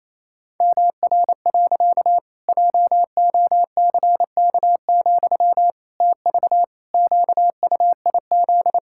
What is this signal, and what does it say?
Morse code, heard as a keyed tone, about 28 wpm, 700 Hz. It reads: MR. JOCK, TV QUIZ
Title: Pangram Challenge v2 - 28wpm